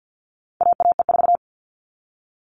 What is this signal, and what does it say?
Morse code, heard as a keyed tone, about 50 wpm, 700 Hz. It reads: AAE4
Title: All Cut Numbers - 4-Digit 50wpm